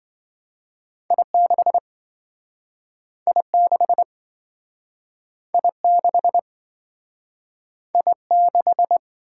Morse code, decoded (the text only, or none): I6 I6 I6 I6